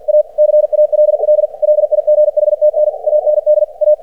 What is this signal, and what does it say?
Morse code, heard as a keyed tone, about 50 wpm, 600 Hz. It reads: A CALL LIGHTN AND A
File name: archived / CRSnet